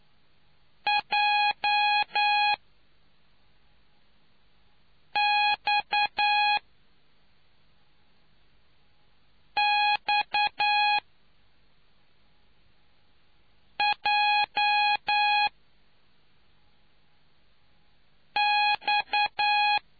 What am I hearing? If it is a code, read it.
Morse code, heard as a keyed tone, about 9 wpm, spaced as for 3 wpm, 830 Hz. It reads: JXXJX